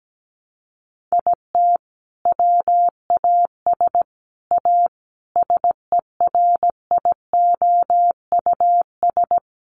Morse code, heard as a keyed tone, about 17 wpm, 700 Hz. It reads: IT WAS A SERIOUS
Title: Sentences from Top 1100 Words 17wpm